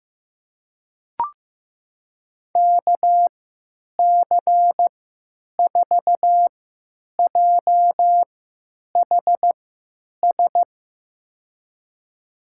Morse code, heard as a keyed tone, about 15 wpm, spaced as for 9 wpm, 700 Hz. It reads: KC4JHS
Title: USA Call Signs - 6 Characters - 3x Character Spacing - 15wpm